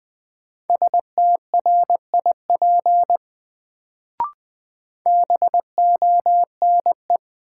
Morse code, heard as a keyed tone, about 20 wpm, 700 Hz. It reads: STRIP BONE
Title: Sentences from Top 1600 Words - New Word Drill - 20wpm